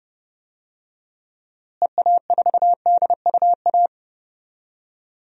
Morse code, heard as a keyed tone, about 30 wpm, 700 Hz. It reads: EA4DUA
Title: DXCC-100 Call Signs - 6 Characters - Repeated 3x - 30wpm